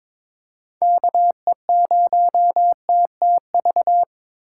Morse code, heard as a keyed tone, about 22 wpm, 700 Hz. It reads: KE0TTV
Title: USA Call Signs - 6x Character Spacing - 22wpm